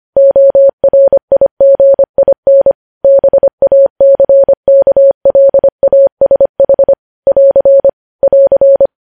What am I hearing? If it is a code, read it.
Morse code, heard as a keyed tone, about 25 wpm, 550 Hz. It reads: ORIGIN BACKLASH <AR> <AR>
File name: podcast / News headlines-1x-25x25